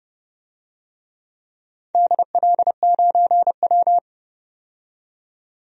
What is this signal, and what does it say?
Morse code, heard as a keyed tone, about 30 wpm, 700 Hz. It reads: DL9W
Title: DXCC-100 Call Signs - 4 Characters - 30wpm